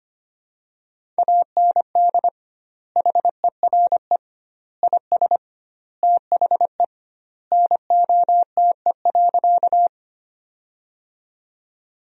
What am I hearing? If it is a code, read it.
Morse code, heard as a keyed tone, about 25 wpm, 700 Hz. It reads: AND HERE IS THE NOTE.
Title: 2x Word-Spacing - Sentences from Top 500 Words - 25wpm